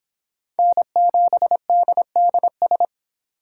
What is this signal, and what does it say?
Morse code, heard as a keyed tone, about 25 wpm, 700 Hz. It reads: N7DDS